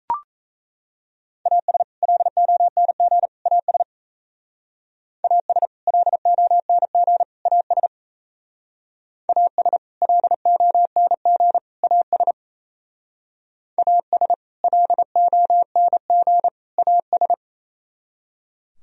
Morse code, heard as a keyed tone, about 34 wpm, 700 Hz. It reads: AS LONG AS AS LONG AS AS LONG AS AS LONG AS
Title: Speed-Racing - Sets of 3 Words 28wpm